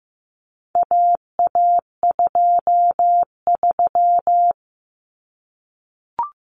Morse code, heard as a keyed tone, about 15 wpm, 700 Hz. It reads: AA23
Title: Common Cut Numbers - 4-Digit 15wpm